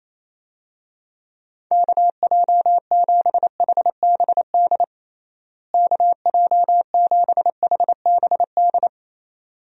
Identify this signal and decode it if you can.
Morse code, heard as a keyed tone, about 28 wpm, 700 Hz. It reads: KJ7HBD KJ7HBD
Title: USA Call Signs - 6 Characters - Repeated 2x - 28wpm